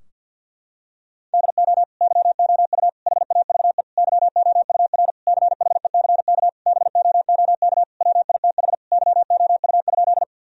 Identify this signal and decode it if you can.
Morse code, heard as a keyed tone, about 50 wpm, 700 Hz. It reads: DO YOU HAVE YOUR CHECK BOOK WITH YOU?
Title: Sentences from Top 1000 Words - Review Part 3 - 50wpm